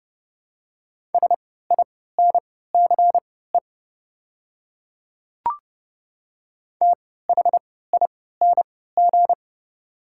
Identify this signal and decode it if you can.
Morse code, heard as a keyed tone, about 30 wpm, spaced as for 17 wpm, 700 Hz. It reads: SINCE THING
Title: 3x Character Spacing - Top 300 Words - 30wpm